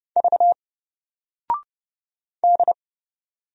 Morse code, heard as a keyed tone, about 30 wpm, 700 Hz. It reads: V D
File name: Single-Letter-Number-30wpm